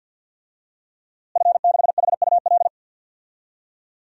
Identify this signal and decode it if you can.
Morse code, heard as a keyed tone, about 50 wpm, 700 Hz. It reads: F6HUL
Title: DXCC-100 Call Signs - 5 Characters - Repeated 2x - 50wpm